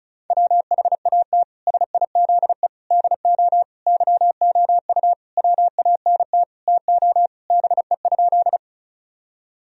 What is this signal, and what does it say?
Morse code, heard as a keyed tone, about 34 wpm, 700 Hz. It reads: WHAT SIZE DO YOU WANT TO BE?
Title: Sentences from Top 1000 Words - Review Part 2 - 35wpm